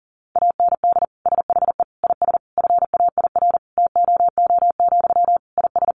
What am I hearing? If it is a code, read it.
Morse code, heard as a keyed tone, about 40 wpm, 700 Hz. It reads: AND SHE IS FAIR TOO, IS
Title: Sentences from Top 1300 Words 40wpm